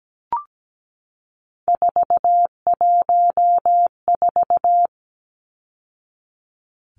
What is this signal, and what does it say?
Morse code, heard as a keyed tone, about 17 wpm, 700 Hz. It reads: 414
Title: Numbers - 3-Digit 17wpm